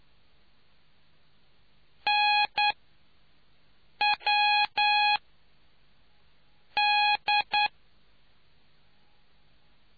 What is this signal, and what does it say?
Morse code, heard as a keyed tone, about 10 wpm, 830 Hz. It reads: N W D